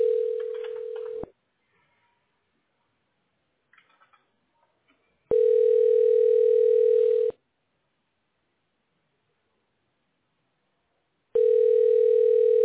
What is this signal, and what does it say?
Ringback tone (North America), 440 + 480 Hz.